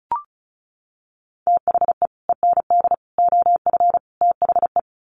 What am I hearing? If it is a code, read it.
Morse code, heard as a keyed tone, about 34 wpm, 700 Hz. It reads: THE END OF THE
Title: Sets of 4 Words 35wpm